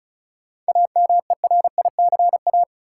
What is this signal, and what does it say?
Morse code, heard as a keyed tone, about 36 wpm, 700 Hz. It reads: AMERICA
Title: Sets of 5 Words - New Word Drill - 35wpm